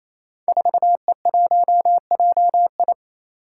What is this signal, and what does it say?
Morse code, heard as a keyed tone, about 29 wpm, 700 Hz. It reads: 4E1JI
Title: Five Challenging Letter-Number Combinations 28wpm